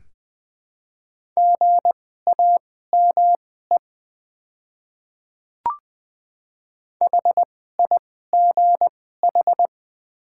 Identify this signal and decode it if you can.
Morse code, heard as a keyed tone, about 20 wpm, spaced as for 14 wpm, 700 Hz. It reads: GAME HIGH
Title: 2x Character Spacing - Top 500 Words - 20wpm